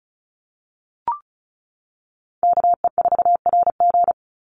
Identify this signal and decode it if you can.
Morse code, heard as a keyed tone, about 36 wpm, 700 Hz. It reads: KE4RG
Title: Call Signs - 35wpm